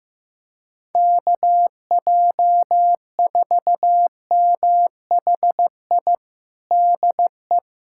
Morse code, heard as a keyed tone, about 15 wpm, 700 Hz. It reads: KJ4MHI DE
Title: USA Callsign DE Callsign - 6 Characters - 15wpm